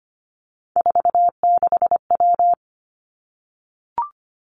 Morse code, heard as a keyed tone, about 25 wpm, 700 Hz. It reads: 46W
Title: Three Challenging Letter-Number Combinations 25wpm